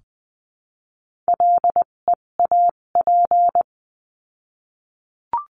Morse code, heard as a keyed tone, about 20 wpm, spaced as for 17 wpm, 700 Hz. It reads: LEAP
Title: 257 Lesson 42b - Words 14-20wpm